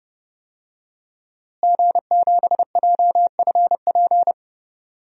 Morse code, heard as a keyed tone, about 30 wpm, 700 Hz. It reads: G7JFP
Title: DXCC-100 Call Signs - 5 Characters - Repeated 2x - 30wpm